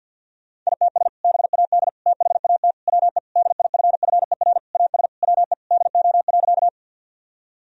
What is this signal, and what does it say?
Morse code, heard as a keyed tone, about 51 wpm, 700 Hz. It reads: ITS BAD THAT WE DIFFER AS WE DO.
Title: Sentences from Top 1000 Words - Review Part 1 - 50wpm